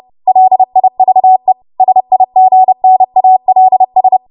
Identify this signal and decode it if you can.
Morse code, heard as a keyed tone, about 30 wpm, 750 Hz. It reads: LIVE SIGNALS
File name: Morse / 30WPM